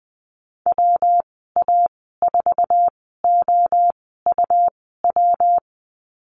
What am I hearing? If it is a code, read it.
Morse code, heard as a keyed tone, about 20 wpm, spaced as for 15 wpm, 700 Hz. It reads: WA4OUW
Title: 167 Lesson 27g - Number 7 - Call Signs 20wpm